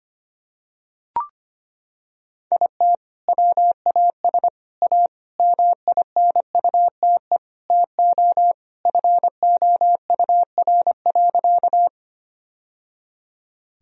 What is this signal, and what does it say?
Morse code, heard as a keyed tone, about 25 wpm, 700 Hz. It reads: IT WAS A MINUTE TO FOUR.
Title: Sentences from Top 1000 Words - Review Part 1 - 25wpm